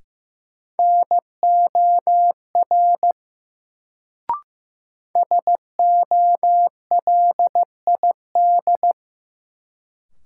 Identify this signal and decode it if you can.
Morse code, heard as a keyed tone, about 16 wpm, 700 Hz. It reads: NOR SOLID
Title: Sentences from Top 1500 Words - New Word Drill - 15wpm